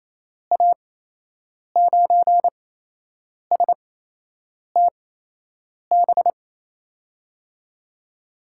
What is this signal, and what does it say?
Morse code, heard as a keyed tone, about 27 wpm, spaced as for 8 wpm, 700 Hz. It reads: A9STB
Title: USA Call Signs - 6 Characters - 8x Character Spacing - 28wpm